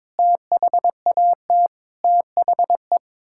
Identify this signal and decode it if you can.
Morse code, heard as a keyed tone, about 22 wpm, 700 Hz. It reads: THAT THE
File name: practice / 22wpm / Sets of 8 Words - 22wpm